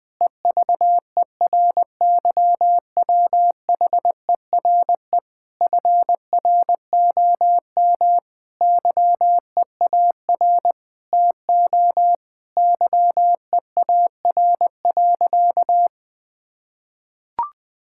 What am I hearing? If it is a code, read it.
Morse code, heard as a keyed tone, about 21 wpm, 700 Hz. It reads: EVERYWHERE FROM YEAR TO YEAR.
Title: Sentences from Top 1600 Words 20wpm